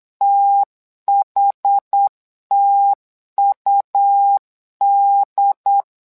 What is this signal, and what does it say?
Morse code, heard as a keyed tone, about 8 wpm, 800 Hz. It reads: THTUD
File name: LEZIONI / 2 UHDN / 2 UHD 8wpm 2258